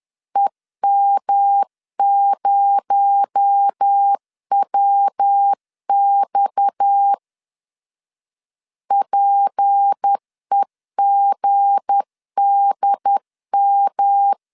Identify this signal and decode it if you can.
Morse code, heard as a keyed tone, about 10 wpm, 800 Hz. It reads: EM0WX PEGDM
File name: LEZIONI / 7 GF9Z / 7 GF 10wpm 2514